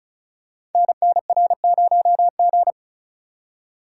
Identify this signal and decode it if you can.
Morse code, heard as a keyed tone, about 36 wpm, 700 Hz. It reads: NNR0G
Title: Code Groups - 5 Letter:Numbers - 35wpm